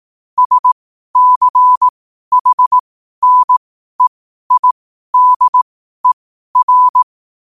Morse code, heard as a keyed tone, about 18 wpm, spaced as for 12 wpm, 1000 Hz. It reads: SCHNEIDER
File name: qotdfiles / 12wpm